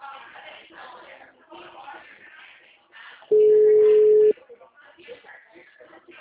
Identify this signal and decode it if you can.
Ringback tone (Europe, 425 Hz), with speech.